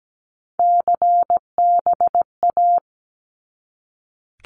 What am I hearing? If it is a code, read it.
Morse code, heard as a keyed tone, about 17 wpm, 700 Hz. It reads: CBA
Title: Abbreviations 17wpm